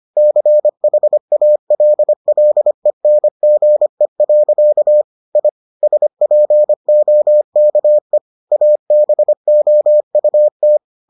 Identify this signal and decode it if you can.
Morse code, heard as a keyed tone, about 25 wpm, 600 Hz. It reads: CHALLENGE. I SPOKE ABOUT